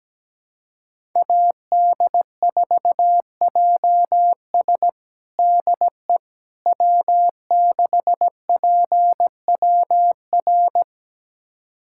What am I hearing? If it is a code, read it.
Morse code, heard as a keyed tone, about 17 wpm, 700 Hz. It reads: AD4JS DE W6PWR
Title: USA Callsign DE Callsign - 5 Characters - 17wpm